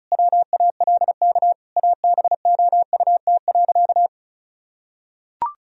Morse code, heard as a keyed tone, about 36 wpm, 700 Hz. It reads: WALK ABOUT.
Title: Sentences from Top 1000 Words - Review Part 1 - 35wpm